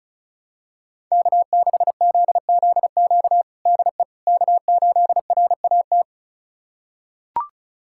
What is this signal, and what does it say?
Morse code, heard as a keyed tone, about 36 wpm, 700 Hz. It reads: K6ZZQ DE K8RAT
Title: USA Callsign DE Callsign 35wpm